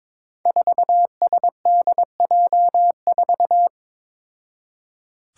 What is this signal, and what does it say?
Morse code, heard as a keyed tone, about 21 wpm, 700 Hz. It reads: 4SDJ4
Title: Five Challenging Letter-Number Combinations 22wpm